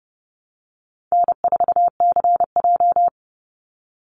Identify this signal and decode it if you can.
Morse code, heard as a keyed tone, about 30 wpm, 700 Hz. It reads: N4CJ
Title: Easy Call Signs - 30wpm